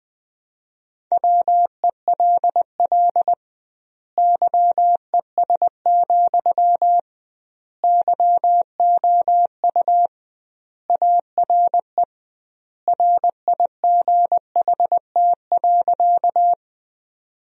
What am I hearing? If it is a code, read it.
Morse code, heard as a keyed tone, about 21 wpm, 700 Hz. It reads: WELL YES, YOU ARE RIGHT.
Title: 2x Word-Spacing - Sentences from Top 500 Words - 20wpm